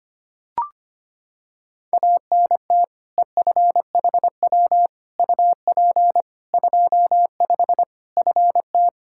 Morse code, heard as a keyed tone, about 25 wpm, 700 Hz. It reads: ANT EFHW UP 25 FT
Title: Practice QSO - Exchange 2 - Info - 25wpm